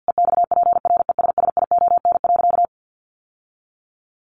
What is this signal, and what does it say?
Morse code, heard as a keyed tone, about 50 wpm, 700 Hz. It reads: EXPRESSION.
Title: Sentences from Top 1200 Words 50wpm